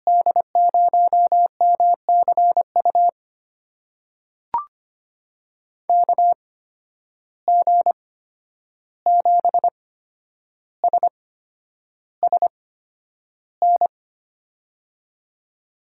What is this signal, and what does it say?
Morse code, heard as a keyed tone, about 25 wpm, spaced as for 10 wpm, 700 Hz. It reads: D0MCU KG7SSN
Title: USA Call Signs - 6 Characters - 8x Character Spacing - 25wpm